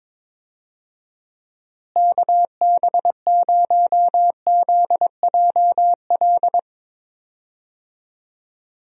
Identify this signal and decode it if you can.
Morse code, heard as a keyed tone, about 22 wpm, 700 Hz. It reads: KB0ZJL
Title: USA Call Signs - 6 Characters - Repeated 2x - 22wpm